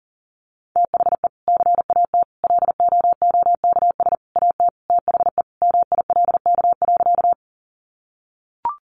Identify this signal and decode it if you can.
Morse code, heard as a keyed tone, about 40 wpm, 700 Hz. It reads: THE CAT LOOKS AT THE MILK.
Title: Additional Easy Phrases 40wpm